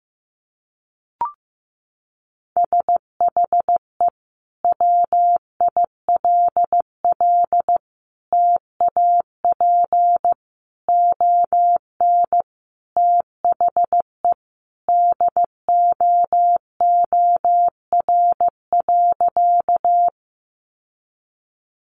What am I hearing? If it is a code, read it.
Morse code, heard as a keyed tone, about 15 wpm, 700 Hz. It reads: SHE WILL TAP ON THE DOOR.
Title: Sentences from Top 1400 Words 15wpm